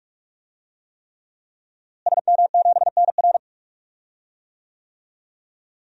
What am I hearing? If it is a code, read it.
Morse code, heard as a keyed tone, about 46 wpm, 700 Hz. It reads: SM7NR
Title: DXCC-100 Call Signs - 5 Characters - Repeated 3x - 45wpm